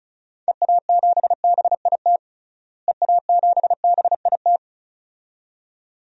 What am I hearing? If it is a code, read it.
Morse code, heard as a keyed tone, about 36 wpm, 700 Hz. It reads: EA7BIT EA7BIT
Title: DXCC-100 Call Signs - 6 Characters - Repeated 2x - 35wpm